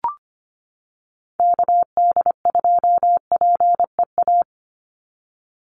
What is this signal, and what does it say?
Morse code, heard as a keyed tone, about 25 wpm, 700 Hz. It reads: KD2PEA
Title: USA Call Signs 25wpm